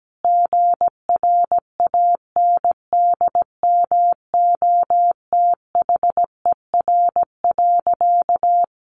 Morse code, heard as a keyed tone, about 17 wpm, 700 Hz. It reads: GRANDMOTHER.
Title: Sentences from Top 1200 Words 17wpm